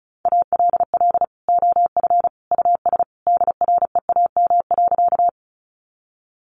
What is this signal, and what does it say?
Morse code, heard as a keyed tone, about 34 wpm, 700 Hz. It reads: ALL OF US DREAM.
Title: Sentences from Top 1500 Words 35wpm